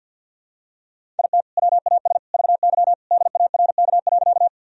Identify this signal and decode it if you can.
Morse code, heard as a keyed tone, about 51 wpm, 700 Hz. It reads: IT WAS VY DARK.
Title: Sentences from Top 500 Words and Abbreviations 50wpm